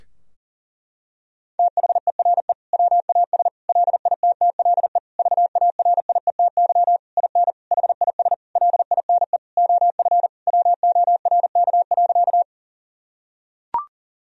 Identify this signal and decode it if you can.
Morse code, heard as a keyed tone, about 40 wpm, 700 Hz. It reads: THERE WAS LITTLE VARIETY IN HIS LINE OF WORK.
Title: Sentences from Top 1000 Words - Review Part 4 - 40wpm